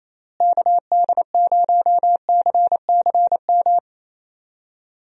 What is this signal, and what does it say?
Morse code, heard as a keyed tone, about 27 wpm, 700 Hz. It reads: KD0CCM
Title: USA Call Signs - 6 Characters - Repeated 2x - 28wpm